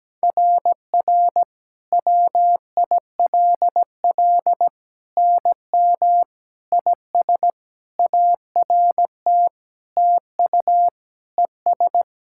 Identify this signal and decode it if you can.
Morse code, heard as a keyed tone, about 17 wpm, 700 Hz. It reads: RR WILL NM IS ART TU ES
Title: SOTA QSO Practice 17wpm